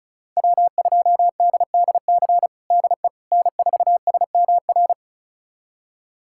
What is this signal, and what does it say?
Morse code, heard as a keyed tone, about 36 wpm, 700 Hz. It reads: W2DDC DE N4SMR
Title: USA Callsign DE Callsign - 5 Characters - 35wpm